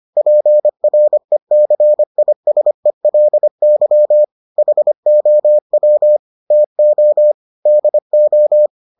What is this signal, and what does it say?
Morse code, heard as a keyed tone, about 25 wpm, 600 Hz. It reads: PRECISELY HOW TO DO